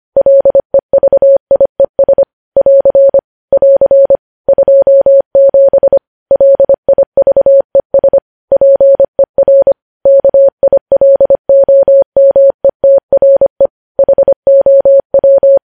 Morse code, heard as a keyed tone, about 25 wpm, 550 Hz. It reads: LEVIES <AR> <AR> 27 LIVES PER KILOMETRE HOW
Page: news-headlines-in-morse-code-at-25-wpm